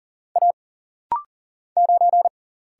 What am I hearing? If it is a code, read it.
Morse code, heard as a keyed tone, about 40 wpm, 700 Hz. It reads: A 9
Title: Single Letter-Number Alt - Rapid-Fire 40wpm